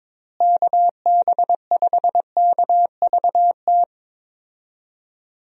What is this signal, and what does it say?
Morse code, heard as a keyed tone, about 22 wpm, 700 Hz. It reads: KB5KVT
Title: USA Call Signs - 6 Characters - Repeated 3x - 22wpm